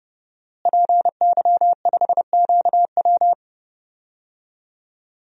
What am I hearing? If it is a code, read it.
Morse code, heard as a keyed tone, about 30 wpm, 700 Hz. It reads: PY5QW
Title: DXCC-100 Call Signs - 5 Characters - Repeated 3x - 30wpm